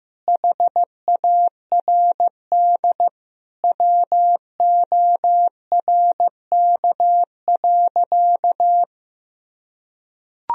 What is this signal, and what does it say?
Morse code, heard as a keyed tone, about 15 wpm, 700 Hz. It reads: HARD WORK.
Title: Sentences from Top 1000 Words - Review Part 3 - 15wpm